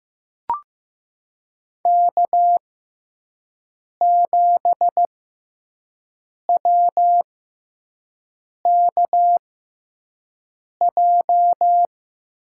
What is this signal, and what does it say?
Morse code, heard as a keyed tone, about 15 wpm, spaced as for 5 wpm, 700 Hz. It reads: K7WKJ
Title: USA Call Signs - 6x Character Spacing - 15wpm